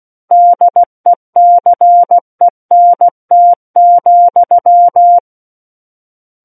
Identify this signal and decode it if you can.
Morse code, heard as a keyed tone, about 16 wpm, 700 Hz. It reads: DECENT,
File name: Book One / 07-Chapter 7 - 16 WPM